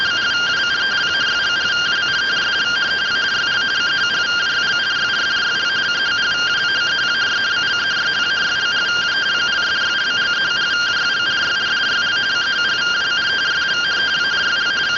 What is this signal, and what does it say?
Two-tone teleprinter signal, about 170 Hz shift.